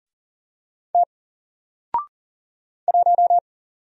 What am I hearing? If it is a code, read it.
Morse code, heard as a keyed tone, about 40 wpm, 700 Hz. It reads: T 1
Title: Single Letter-Number 40wpm